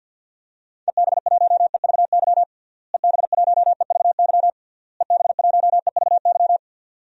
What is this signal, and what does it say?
Morse code, heard as a keyed tone, about 51 wpm, 700 Hz. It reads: EB1EVY EB1EVY EB1EVY
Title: DXCC-100 Call Signs - 6 Characters - Repeated 3x - 50wpm